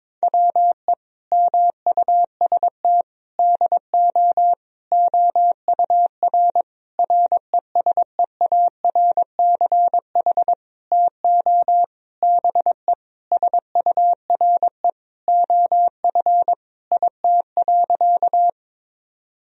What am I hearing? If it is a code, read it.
Morse code, heard as a keyed tone, about 22 wpm, 700 Hz. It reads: WE MUST DO OUR RESEARCH TO BE SURE OF IT.
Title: Sentences from Top 1000 Words - Review Part 4 - 22wpm